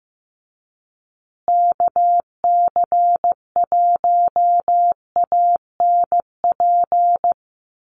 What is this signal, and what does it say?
Morse code, heard as a keyed tone, about 15 wpm, 700 Hz. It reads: KC1ANP
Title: USA Call Signs Repeated 2x 15wpm